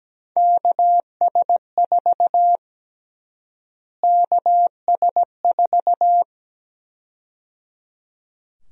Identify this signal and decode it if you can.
Morse code, heard as a keyed tone, about 17 wpm, 700 Hz. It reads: KS4 KS4
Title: USA Call Sign Prefixes - Repeated 2x - 17wpm